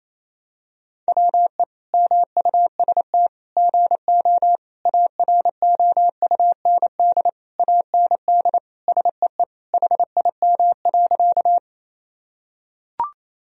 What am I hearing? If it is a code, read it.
Morse code, heard as a keyed tone, about 28 wpm, 700 Hz. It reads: WE MUST GO AROUND AND SEE HIM.
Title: Sentences from Top 1000 Words - Review Part 3 - 28wpm